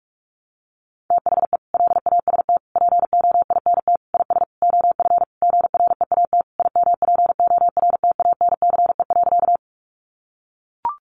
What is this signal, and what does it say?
Morse code, heard as a keyed tone, about 44 wpm, 700 Hz. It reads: THE LAST POINT IS OF GREAT IMPORTANCE.
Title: Sentences from Top 1200 Words 45wpm